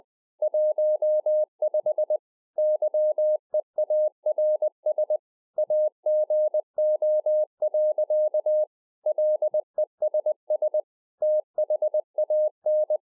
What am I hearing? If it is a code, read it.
Morse code, heard as a keyed tone, about 20 wpm, 610 Hz. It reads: 15 YEARS AGO. LESS THAN